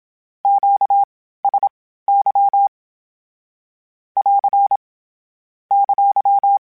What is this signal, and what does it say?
Morse code, heard as a keyed tone, about 27 wpm, spaced as for 15 wpm, 800 Hz. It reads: QSY <AR> !
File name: (spaced) / Q-Calls Spaced-mp3s / Q Calls Twice sp @25